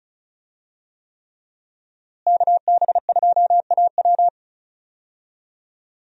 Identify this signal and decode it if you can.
Morse code, heard as a keyed tone, about 36 wpm, 700 Hz. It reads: KB2AW
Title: USA Call Signs - 5 Characters - Repeated 3x - 35wpm